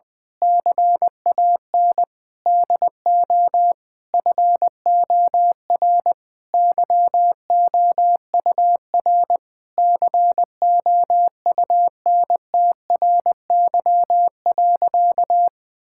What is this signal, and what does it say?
Morse code, heard as a keyed tone, about 21 wpm, 700 Hz. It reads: CAN DO FOR YOUR COUNTRY.
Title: Sentences with 9-Word Phrases - 20wpm